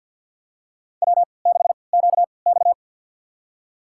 Morse code, heard as a keyed tone, about 51 wpm, spaced as for 28 wpm, 700 Hz. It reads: W6QX
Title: USA Call Signs - 4 Characters - 3x Character Spacing - 50wpm